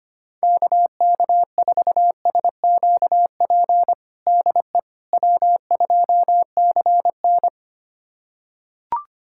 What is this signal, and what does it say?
Morse code, heard as a keyed tone, about 25 wpm, 700 Hz. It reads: KK4SQP DE W2CN
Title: USA Callsign DE Callsign 25wpm